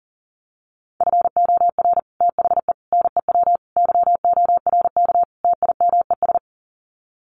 Morse code, heard as a keyed tone, about 40 wpm, 700 Hz. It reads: FOR THE NEW YORK TIMES
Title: Sets of 5 Words 40wpm